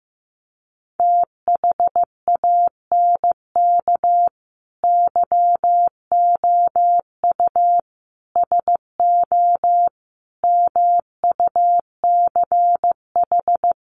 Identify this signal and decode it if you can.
Morse code, heard as a keyed tone, about 15 wpm, 700 Hz. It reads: THANK YOU SO MUCH
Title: Sets of 5 Words 15wpm